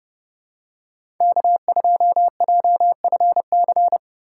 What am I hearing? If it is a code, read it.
Morse code, heard as a keyed tone, about 30 wpm, 690 Hz. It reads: K2JFC
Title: USA Call Signs - 4x Character Spacing - 30wpm